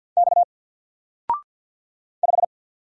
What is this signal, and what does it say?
Morse code, heard as a keyed tone, about 49 wpm, 700 Hz. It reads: X 5
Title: Single Letter-Number Alt 50wpm